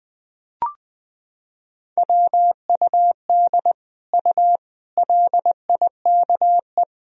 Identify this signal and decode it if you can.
Morse code, heard as a keyed tone, about 21 wpm, 700 Hz. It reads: WUD U LIKE
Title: Sentences from Top 100 Words and Abbreviations 20wpm